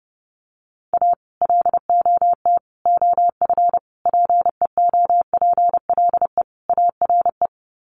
Morse code, heard as a keyed tone, about 30 wpm, 700 Hz. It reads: A LOT OF PEOPLE ARE
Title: Sets of 5 Words 30wpm